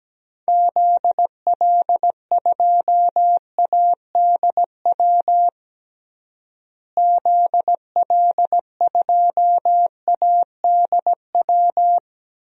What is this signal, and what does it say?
Morse code, heard as a keyed tone, about 17 wpm, 700 Hz. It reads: ZL2ADW ZL2ADW
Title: DXCC-100 Call Signs - 6 Characters - Repeated 3x - 17wpm